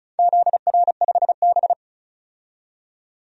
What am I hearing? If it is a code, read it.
Morse code, heard as a keyed tone, about 34 wpm, 700 Hz. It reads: ZR5B
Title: DXCC-100 Call Signs - 4 Characters - 35wpm